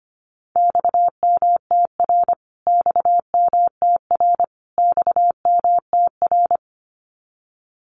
Morse code, heard as a keyed tone, about 25 wpm, 700 Hz. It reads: XMTR XMTR XMTR
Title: Abbreviations Repeated 3x - 25wpm